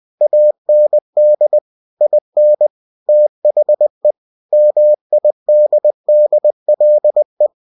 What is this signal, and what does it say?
Morse code, heard as a keyed tone, about 20 wpm, 600 Hz. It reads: AND IN THE MIDDLE
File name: cw-books / winnie-the-pooh / winnie-20wpm-page-0002